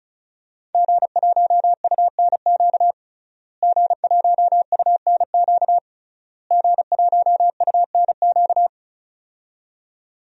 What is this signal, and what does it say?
Morse code, heard as a keyed tone, about 36 wpm, 700 Hz. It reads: G1UNQ G1UNQ G1UNQ
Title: DXCC-100 Call Signs - 5 Characters - Repeated 3x - 35wpm